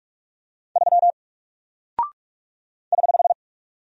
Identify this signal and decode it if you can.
Morse code, heard as a keyed tone, about 44 wpm, 700 Hz. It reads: 3 <HH>
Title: Single Character (Letter:Number:Punctuation:Prosign) 45wpm